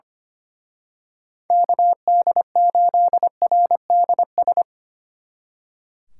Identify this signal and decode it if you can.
Morse code, heard as a keyed tone, about 27 wpm, 700 Hz. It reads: KD8RDS
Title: Speed Racing - USA 2x3 Call Signs 25wpm